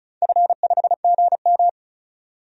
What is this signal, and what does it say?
Morse code, heard as a keyed tone, about 36 wpm, 700 Hz. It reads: F5GM
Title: DXCC-100 Call Signs - 4 Characters - 35wpm